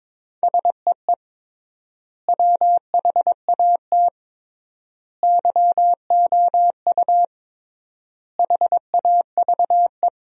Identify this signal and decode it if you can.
Morse code, heard as a keyed tone, about 22 wpm, 700 Hz. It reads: SEE WHAT YOU HAVE
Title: 3x Word-Spacing - Sentences from Top 500 Words - 22wpm